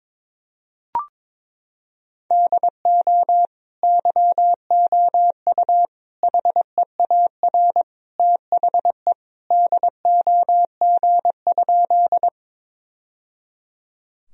Morse code, heard as a keyed tone, about 21 wpm, 700 Hz. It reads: DO YOU HEAR THE DOG?
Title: Sentences from Top 1000 Words - Review Part 2 - 22wpm